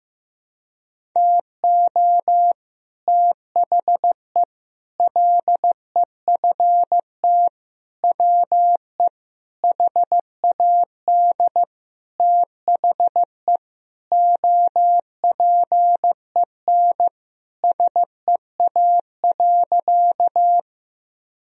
Morse code, heard as a keyed tone, about 15 wpm, 700 Hz. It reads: TO THE LEFT WE HAD THE OPEN SEA.
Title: Sentences from Top 1000 Words - Review Part 1 - 15wpm